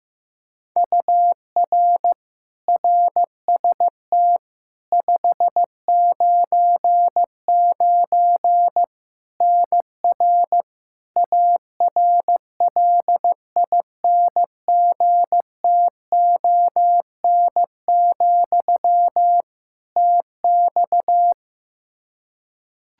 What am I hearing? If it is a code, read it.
Morse code, heard as a keyed tone, about 15 wpm, 700 Hz. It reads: UR RST 599 NR ARLINGTON, TX
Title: Practice QSO - Exchange 1 - Beginning - 15wpm